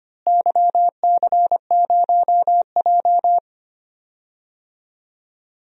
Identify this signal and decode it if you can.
Morse code, heard as a keyed tone, about 25 wpm, 700 Hz. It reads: YC0J
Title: DXCC-100 Call Signs - 4 Characters - Repeated 3x - 25wpm